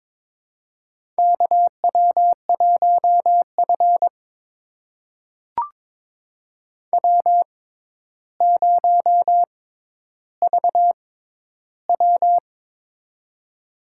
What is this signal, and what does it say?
Morse code, heard as a keyed tone, about 22 wpm, spaced as for 11 wpm, 700 Hz. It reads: KW1F W0VW
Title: USA Call Signs - 4 Characters - 6x Character Spacing - 22wpm